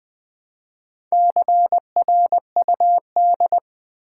Morse code, heard as a keyed tone, about 20 wpm, 700 Hz. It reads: CRUD
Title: Build A Word - 1 Letter At A Time - 5-Letter Words - 20wpm